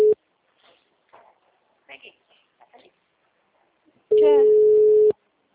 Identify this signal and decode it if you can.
Ringback tone (Europe, 425 Hz), with speech.